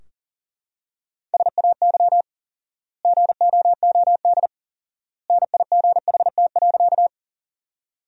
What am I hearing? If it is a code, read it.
Morse code, heard as a keyed tone, about 40 wpm, 700 Hz. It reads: SAY GOOD NIGHT.
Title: 4x Word-Spacing - Sentences from Top 300 Words - 40wpm